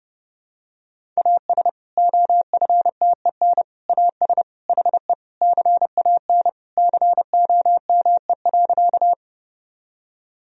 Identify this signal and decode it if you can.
Morse code, heard as a keyed tone, about 32 wpm, 700 Hz. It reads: AS OFTEN AS HE CAN COME.
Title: Sentences from Top 1000 Words - Review Part 1 - 30wpm